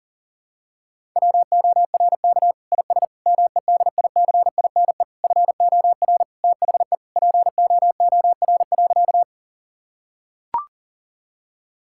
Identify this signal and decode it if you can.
Morse code, heard as a keyed tone, about 40 wpm, 700 Hz. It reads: WORK IS MEDICINE FOR THE POOR.
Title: Sentences from Top 1000 Words - Review Part 6 - 40wpm